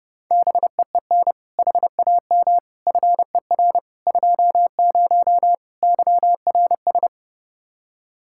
Morse code, heard as a keyed tone, about 30 wpm, 700 Hz. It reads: BEEN HAM FER 20 YRS
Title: Practice QSO - Exchange 3 - More Info - 30wpm